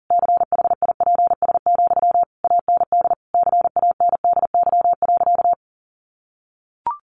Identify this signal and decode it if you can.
Morse code, heard as a keyed tone, about 40 wpm, 700 Hz. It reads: CHIPS, AND CANDY.
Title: Easy Phrases 40wpm